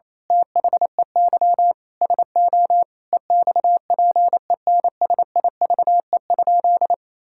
Morse code, heard as a keyed tone, about 28 wpm, 700 Hz. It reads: THEY SO EXPENSIVE?
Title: Sentences from Top 1000 Words - Review Part 6 - 28wpm